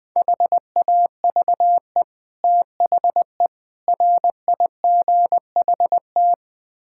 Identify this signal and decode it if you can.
Morse code, heard as a keyed tone, about 21 wpm, 700 Hz. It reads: HAVE THE RIGHT
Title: Build A Phrase - 4-Word Phrases - 20wpm